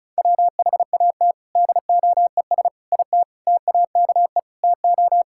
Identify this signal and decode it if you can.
Morse code, heard as a keyed tone, about 36 wpm, 700 Hz. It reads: WHAT DOES IT TAKE TO
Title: Sentences from Top 1600 Words 35wpm